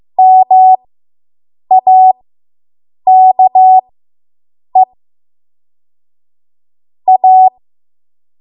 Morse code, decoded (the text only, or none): MAKE A